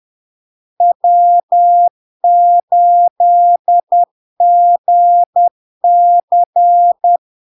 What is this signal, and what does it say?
Morse code, heard as a keyed tone, about 10 wpm, 700 Hz. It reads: W8GC
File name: LARK-FD-2022-Log-in-CW-10wpm-700-Hz